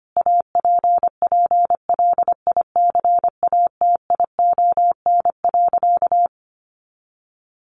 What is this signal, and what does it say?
Morse code, heard as a keyed tone, about 25 wpm, 700 Hz. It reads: APPLICATION.
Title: Sentences from Top 1200 Words 25wpm